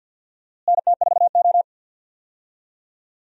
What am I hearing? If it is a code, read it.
Morse code, heard as a keyed tone, about 51 wpm, 700 Hz. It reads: NT4O
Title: USA Call Signs - 4 Characters - Repeated 2x - 50wpm